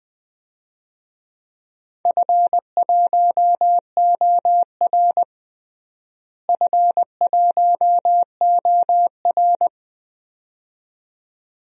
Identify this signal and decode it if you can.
Morse code, heard as a keyed tone, about 20 wpm, 700 Hz. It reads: F1OR F1OR
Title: DXCC-100 Call Signs - 4 Characters - Repeated 2x - 20wpm